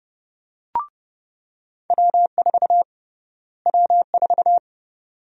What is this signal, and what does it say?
Morse code, heard as a keyed tone, about 30 wpm, 700 Hz. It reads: W4 W4
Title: USA Call Sign Prefixes - Repeated 2x - 30wpm